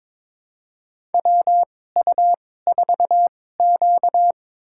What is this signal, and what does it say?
Morse code, heard as a keyed tone, about 22 wpm, spaced as for 16 wpm, 700 Hz. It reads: WU4Q
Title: USA Call Signs - 4 Characters - 2x Character Spacing - 22wpm